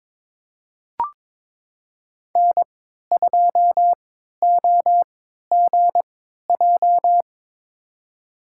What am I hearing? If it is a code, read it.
Morse code, heard as a keyed tone, about 22 wpm, spaced as for 12 wpm, 700 Hz. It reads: N2OGJ
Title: USA Call Signs - 3x Character Spacing - 22wpm